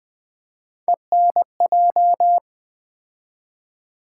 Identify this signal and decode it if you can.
Morse code, heard as a keyed tone, about 20 wpm, 700 Hz. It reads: ENJ
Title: Build A Word - 3 Letters At A Time - 9-Letter Words - 20wpm